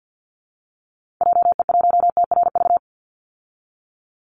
Morse code, heard as a keyed tone, about 48 wpm, 700 Hz. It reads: JE1TRV
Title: Call Signs Repeated 3x - 50wpm